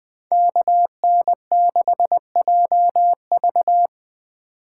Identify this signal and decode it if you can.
Morse code, heard as a keyed tone, about 20 wpm, 700 Hz. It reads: KN6JV